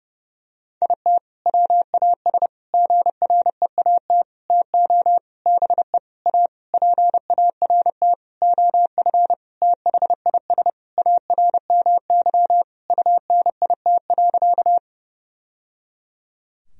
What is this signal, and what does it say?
Morse code, heard as a keyed tone, about 32 wpm, 700 Hz. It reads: IT WAS GREAT TO BE A PART OF THIS ARMY UNIT.
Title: Sentences from Top 1000 Words - Review Part 2 - 30wpm